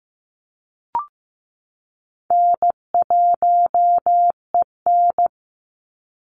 Morse code, heard as a keyed tone, about 15 wpm, 700 Hz. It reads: N1EN
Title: Easy Call Signs Repeated 2x - 15wpm